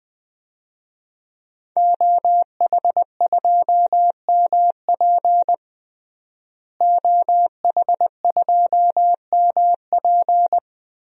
Morse code, heard as a keyed tone, about 21 wpm, 700 Hz. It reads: OH2MP OH2MP
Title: DXCC-100 Call Signs - 5 Characters - Repeated 2x - 20wpm